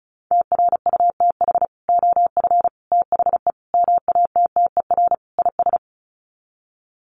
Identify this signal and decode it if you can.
Morse code, heard as a keyed tone, about 34 wpm, 700 Hz. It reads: TRUTH OF THE MATTER IS
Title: Sets of 5 Words Encore2 35wpm